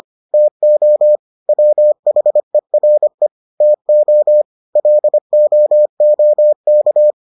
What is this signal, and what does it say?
Morse code, heard as a keyed tone, about 25 wpm, 600 Hz. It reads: TO WHERE TO LOOK